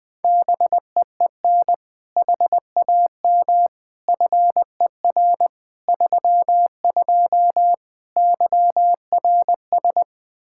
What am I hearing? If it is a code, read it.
Morse code, heard as a keyed tone, about 20 wpm, 700 Hz. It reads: BEEN HAM FER 32 YRS
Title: K1USN SST QSO Practice 20wpm